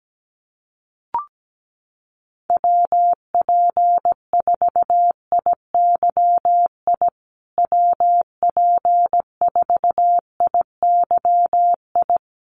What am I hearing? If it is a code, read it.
Morse code, heard as a keyed tone, about 17 wpm, 700 Hz. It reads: WP4IYI WP4IYI
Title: USA Call Signs Repeated 3x 17wpm